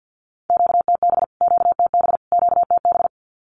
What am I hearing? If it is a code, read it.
Morse code, heard as a keyed tone, about 48 wpm, 700 Hz. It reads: QTB QTB QTB
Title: Q-Codes Repeated 3x 50wpm